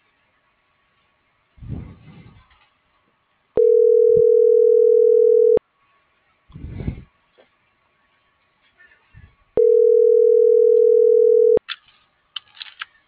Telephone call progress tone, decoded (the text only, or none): ringback tone